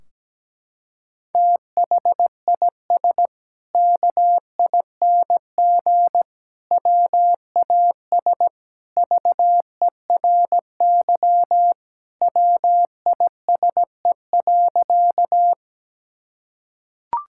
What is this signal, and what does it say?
Morse code, heard as a keyed tone, about 17 wpm, 700 Hz. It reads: THIS KING WAS VERY WISE.
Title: Sentences from Top 1600 Words 17wpm